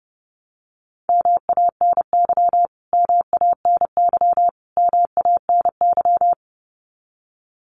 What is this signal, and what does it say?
Morse code, heard as a keyed tone, about 30 wpm, 700 Hz. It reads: MANY MANY MANY
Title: Top 300 Words Repeated 30wpm